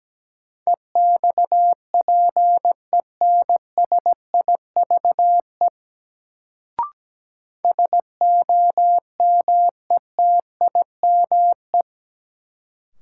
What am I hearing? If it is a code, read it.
Morse code, heard as a keyed tone, about 17 wpm, 700 Hz. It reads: EXPENSIVE SOMETIME
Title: Sentences from Top 1500 Words - New Word Drill Alt - 17wpm